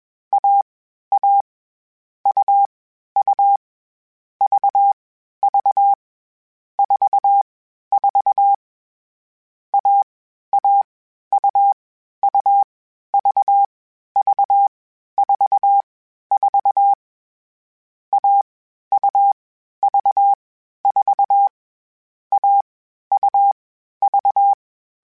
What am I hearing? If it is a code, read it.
Morse code, heard as a keyed tone, about 21 wpm, 800 Hz. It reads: A A U U V V 4 4 A A U U V V 4 4 A U V 4 A U V
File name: CW Code Groups-mp3s / Sets mp3s / AUV AUV4 sp @20